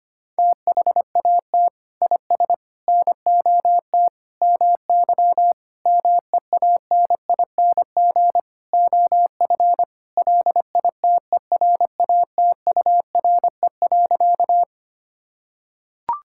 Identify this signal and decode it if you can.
Morse code, heard as a keyed tone, about 25 wpm, 700 Hz. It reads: THAT IS NOT MY MEANING OF LITERATURE.
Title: Sentences from Top 1000 Words - Review Part 6 - 25wpm